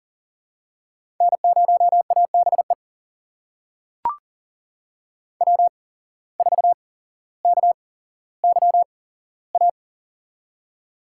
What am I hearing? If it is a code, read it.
Morse code, heard as a keyed tone, about 39 wpm, spaced as for 17 wpm, 700 Hz. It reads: N0ABE W4KYA
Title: USA Call Signs - 5 Characters - 8x Character Spacing - 40wpm